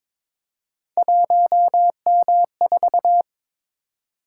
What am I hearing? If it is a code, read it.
Morse code, heard as a keyed tone, about 23 wpm, 700 Hz. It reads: 1M4
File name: practice / 22wpm / Three Letter-Number Combinations - 22wpm